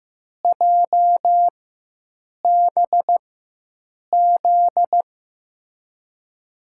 Morse code, heard as a keyed tone, about 15 wpm, spaced as for 7 wpm, 700 Hz. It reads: JBZ
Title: USA Call Signs - 4x Character Spacing - 15wpm